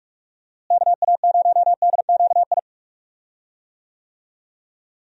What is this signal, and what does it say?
Morse code, heard as a keyed tone, about 45 wpm, 700 Hz. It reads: KA0DQI
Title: USA Call Signs - 6 Characters - Repeated 2x - 45wpm